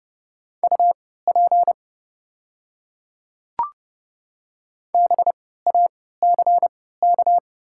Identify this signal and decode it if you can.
Morse code, heard as a keyed tone, about 32 wpm, spaced as for 17 wpm, 700 Hz. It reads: UP BACK
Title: 3x Character Spacing - Top 200 Words - 30wpm